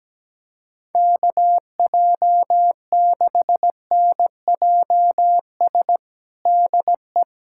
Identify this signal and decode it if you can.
Morse code, heard as a keyed tone, about 17 wpm, 700 Hz. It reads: KJ6NJS DE
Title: USA Callsign DE Callsign 17wpm